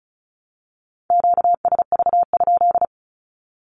Morse code, heard as a keyed tone, about 34 wpm, 700 Hz. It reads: QSV?
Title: Q-Codes Repeated 3x 35wpm